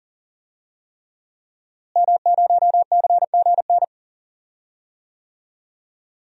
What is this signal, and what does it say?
Morse code, heard as a keyed tone, about 40 wpm, 700 Hz. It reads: M0CGN
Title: DXCC-100 Call Signs - 5 Characters - Repeated 3x - 40wpm